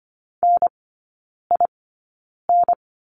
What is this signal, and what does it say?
Morse code, heard as a keyed tone, about 25 wpm, spaced as for 9 wpm, 700 Hz. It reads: NIN
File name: course / 25:10wpm / 093 Lesson 16e - Letter c - Three Characters 25wpm